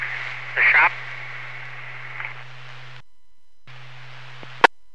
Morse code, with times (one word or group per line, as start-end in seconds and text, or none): none